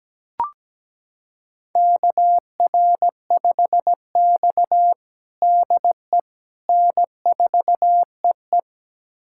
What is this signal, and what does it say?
Morse code, heard as a keyed tone, about 17 wpm, 700 Hz. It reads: KR5X DE N4EE
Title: USA Callsign DE Callsign - 4 Characters - 17wpm